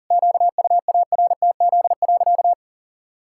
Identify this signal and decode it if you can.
Morse code, heard as a keyed tone, about 40 wpm, 700 Hz. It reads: QUARTZ.
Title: Pangram Challenge v2 - 40wpm